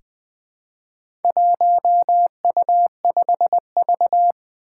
Morse code, heard as a keyed tone, about 20 wpm, 700 Hz. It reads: 1U5V
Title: Four Challenging Letter-Number Combinations 20wpm